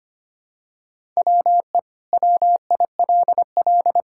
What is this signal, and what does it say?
Morse code, heard as a keyed tone, about 25 wpm, 700 Hz. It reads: WE WILL
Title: Sentences from Top 1000 Words - Review Part 5 - 25wpm